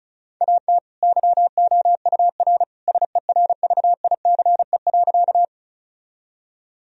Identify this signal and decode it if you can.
Morse code, heard as a keyed tone, about 36 wpm, 700 Hz. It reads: AT YOUR SERVICE.
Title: Sentences from Top 1000 Words - Review Part 4 - 35wpm